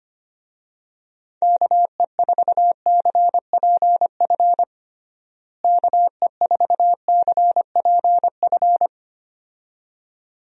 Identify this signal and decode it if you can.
Morse code, heard as a keyed tone, about 25 wpm, 700 Hz. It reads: KE4CPF KE4CPF
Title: USA Call Signs - 6 Characters - Repeated 2x - 25wpm